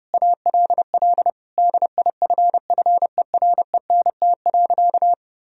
Morse code, heard as a keyed tone, about 32 wpm, 700 Hz. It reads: ALL DIFFERENT.
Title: Sentences from Top 1000 Words - Review Part 3 - 30wpm